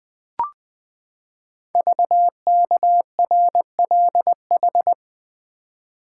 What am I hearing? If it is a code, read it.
Morse code, heard as a keyed tone, about 20 wpm, 700 Hz. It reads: VKRLH
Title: Code Groups - 5 Letters - 20wpm